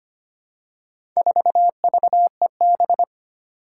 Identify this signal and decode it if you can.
Morse code, heard as a keyed tone, about 25 wpm, 700 Hz. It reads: 4VEB
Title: Four Challenging Letter-Number Combinations 25wpm